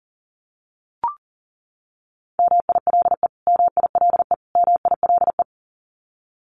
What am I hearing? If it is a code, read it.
Morse code, heard as a keyed tone, about 40 wpm, 700 Hz. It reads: MILE MILE MILE
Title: Top 300 Words Repeated 40wpm